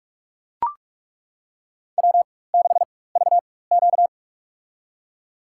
Morse code, heard as a keyed tone, about 43 wpm, spaced as for 21 wpm, 700 Hz. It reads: W6VQ
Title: USA Call Signs - 4 Characters - 4x Character Spacing - 45wpm